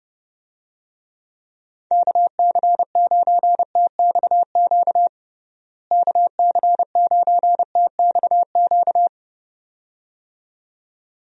Morse code, heard as a keyed tone, about 32 wpm, 700 Hz. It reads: KC9TXQ KC9TXQ
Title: USA Call Signs - 6 Characters - Repeated 2x - 30wpm